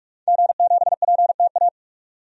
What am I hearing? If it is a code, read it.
Morse code, heard as a keyed tone, about 47 wpm, 700 Hz. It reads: G7PTA
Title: DXCC-100 Call Signs - 5 Characters - 45wpm